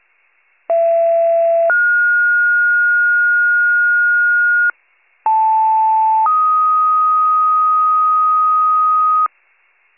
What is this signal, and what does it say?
Two-tone page, heard as a one-second tone, then a three-second tone, about 670 then 1430 Hz, about 880 then 1250 Hz, with noise. It is heard from 0.7 to 9.3 s.